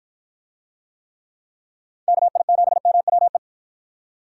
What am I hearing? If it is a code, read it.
Morse code, heard as a keyed tone, about 52 wpm, 700 Hz. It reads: KI7MWE